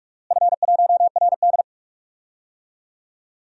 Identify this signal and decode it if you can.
Morse code, heard as a keyed tone, about 42 wpm, 700 Hz. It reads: F1RD
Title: DXCC-100 Call Signs - 4 Characters - Repeated 3x - 45wpm